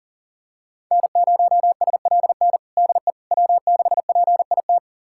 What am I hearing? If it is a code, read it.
Morse code, heard as a keyed tone, about 37 wpm, 700 Hz. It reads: N0SLN DE W6PIT
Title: USA Callsign DE Callsign - 5 Characters - 40wpm